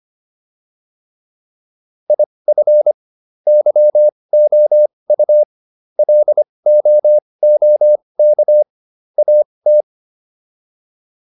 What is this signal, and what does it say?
Morse code, heard as a keyed tone, about 25 wpm, spaced as for 20 wpm, 600 Hz. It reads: IF YOU LOOK AT